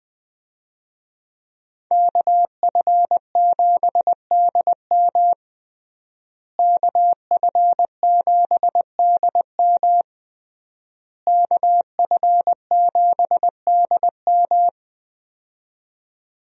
Morse code, heard as a keyed tone, about 21 wpm, 700 Hz. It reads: KF7DM KF7DM KF7DM
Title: USA Call Signs - 5 Characters - Repeated 3x - 20wpm